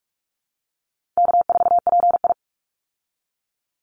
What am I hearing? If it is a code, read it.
Morse code, heard as a keyed tone, about 44 wpm, 700 Hz. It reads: K4PI
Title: Call Signs Repeated 3x - 45wpm